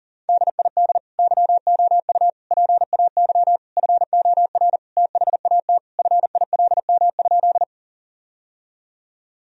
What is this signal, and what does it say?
Morse code, heard as a keyed tone, about 40 wpm, 700 Hz. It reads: DID YOU PAY FOR THAT FILM?
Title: Sentences from Top 1000 Words - Review Part 3 - 40wpm